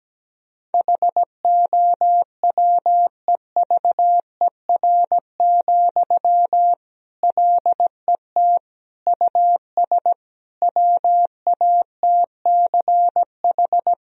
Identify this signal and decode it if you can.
Morse code, heard as a keyed tone, about 17 wpm, 700 Hz. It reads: HOWEVER, LET US WATCH
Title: Sentences from Top 1000 Words - Review Part 1 - 17wpm